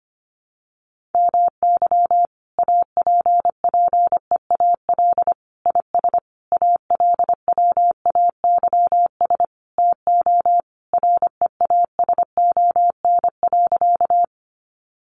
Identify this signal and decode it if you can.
Morse code, heard as a keyed tone, about 25 wpm, 700 Hz. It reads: MY APPEAL IS ALWAYS TO REASON.
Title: Sentences from Top 1300 Words 25wpm